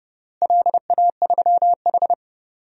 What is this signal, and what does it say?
Morse code, heard as a keyed tone, about 30 wpm, 700 Hz. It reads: LA3H
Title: DXCC-100 Call Signs - 4 Characters - Repeated 3x - 30wpm